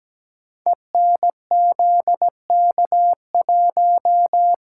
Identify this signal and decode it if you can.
Morse code, heard as a keyed tone, about 17 wpm, 700 Hz. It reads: ENZK1
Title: Code Groups - 5 Letter:Numbers - 17wpm